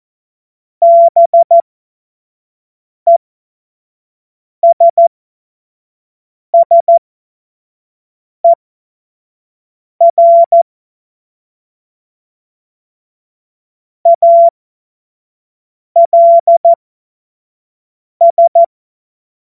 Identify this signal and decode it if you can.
Morse code, heard as a keyed tone, about 14 wpm, spaced as for 5 wpm, 680 Hz. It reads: BESSER ALS
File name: Morsekurs / Witze / Witz 21. Dezember 2021, 50 F70